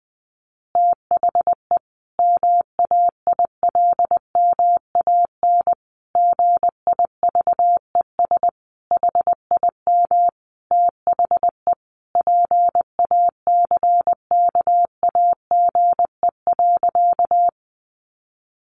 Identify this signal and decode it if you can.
Morse code, heard as a keyed tone, about 19 wpm, 700 Hz. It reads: THE MAILMAN GIVES HIM THE PACKAGE.
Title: Additional Easy Phrases 20wpm